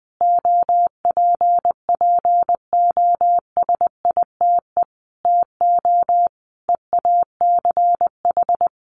Morse code, heard as a keyed tone, about 20 wpm, 700 Hz. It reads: OPPOSITE TO EACH
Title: Sentences from Top 1300 Words 20wpm